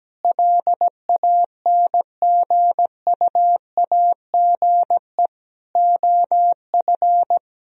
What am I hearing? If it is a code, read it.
Morse code, heard as a keyed tone, about 17 wpm, 700 Hz. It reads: LANGUAGE OF
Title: Sentences from Top 1000 Words - Review Part 3 - 17wpm